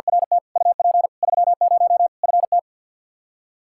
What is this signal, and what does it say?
Morse code, heard as a keyed tone, about 51 wpm, 700 Hz. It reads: RT UP 30 FT
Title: K1USN SST QSO Practice 50wpm